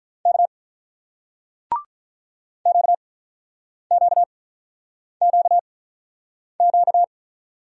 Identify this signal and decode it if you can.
Morse code, heard as a keyed tone, about 45 wpm, 700 Hz. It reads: K Q Q Q Q